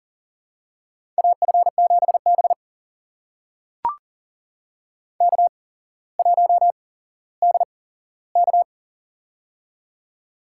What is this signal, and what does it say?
Morse code, heard as a keyed tone, about 42 wpm, spaced as for 17 wpm, 700 Hz. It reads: AF7B K1DK
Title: USA Call Signs - 4 Characters - 8x Character Spacing - 40wpm